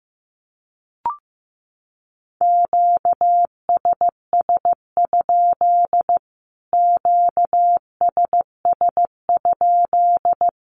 Morse code, heard as a keyed tone, about 15 wpm, 700 Hz. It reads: QSS? QSS?
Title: Q-Codes Repeated 3x 15wpm